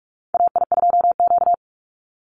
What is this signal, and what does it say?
Morse code, heard as a keyed tone, about 44 wpm, 700 Hz. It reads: AI2Q
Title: Call Signs Repeated 3x - 45wpm